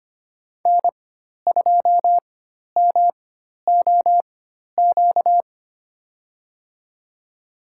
Morse code, heard as a keyed tone, about 25 wpm, spaced as for 12 wpm, 700 Hz. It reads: N2MOQ
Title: USA Call Signs - 5 Characters - 4x Character Spacing - 25wpm